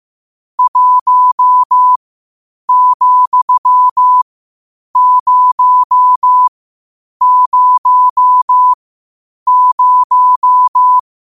Morse code, heard as a keyed tone, about 15 wpm, spaced as for 8 wpm, 1000 Hz. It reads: 1,000
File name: qotdfiles / 8wpm